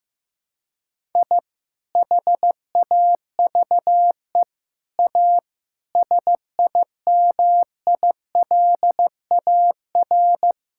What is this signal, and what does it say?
Morse code, heard as a keyed tone, about 16 wpm, 700 Hz. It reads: I HAVE A SIMILAR
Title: Sentences from Top 1000 Words - Review Part 5 - 15wpm